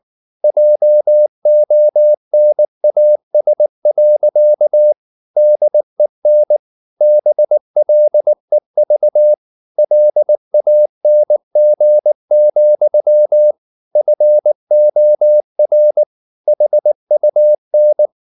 Morse code, heard as a keyed tone, about 19 wpm, 600 Hz. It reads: JONAS. DEN BLEV LANG, FOR HUN
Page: Ferie 19 wpm